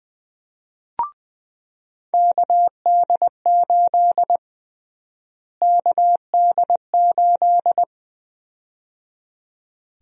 Morse code, heard as a keyed tone, about 20 wpm, 700 Hz. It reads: KD8 KD8
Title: USA Call Sign Prefixes - Repeated 2x - 20wpm